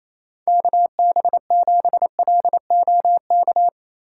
Morse code, heard as a keyed tone, about 28 wpm, 700 Hz. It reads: KB7LOK
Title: USA Call Signs - 2x Character Spacing - 28wpm